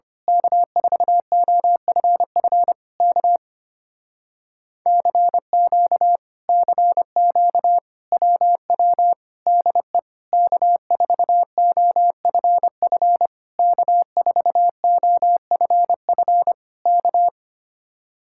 Morse code, heard as a keyed tone, about 24 wpm, 700 Hz. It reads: K4OFF K CQ CQ WW DE K4OFF K4OFF K
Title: Speed-Racing Common QSO Phrases - 25wpm